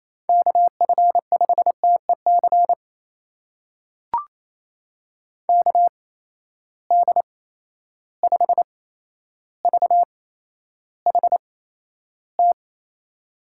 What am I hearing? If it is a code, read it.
Morse code, heard as a keyed tone, about 28 wpm, spaced as for 12 wpm, 700 Hz. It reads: KF5TEC KD5VHT
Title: USA Call Signs - 6 Characters - 8x Character Spacing - 28wpm